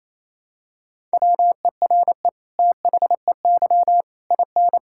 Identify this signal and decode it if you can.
Morse code, heard as a keyed tone, about 28 wpm, 700 Hz. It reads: WERE THEY IN
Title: Sentences from Top 1000 Words - Review Part 6 - 28wpm